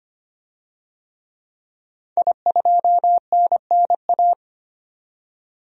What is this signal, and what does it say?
Morse code, heard as a keyed tone, about 25 wpm, 700 Hz. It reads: I2NNA
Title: DXCC-100 Call Signs - 5 Characters - 25wpm